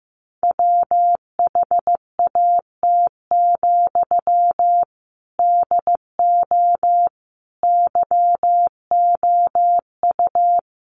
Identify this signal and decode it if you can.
Morse code, heard as a keyed tone, about 15 wpm, 700 Hz. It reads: WHAT, DO YOU
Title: Sentences from Top 600 Words 15wpm